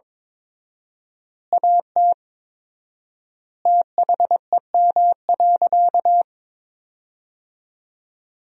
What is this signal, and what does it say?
Morse code, heard as a keyed tone, about 22 wpm, 700 Hz. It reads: AT THEM.
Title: 4x Word-Spacing - Sentences from Top 100 Words - 22wpm